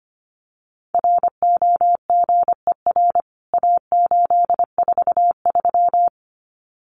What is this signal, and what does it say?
Morse code, heard as a keyed tone, about 25 wpm, 700 Hz. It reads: ROGER A843
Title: CWT Exchange 25wpm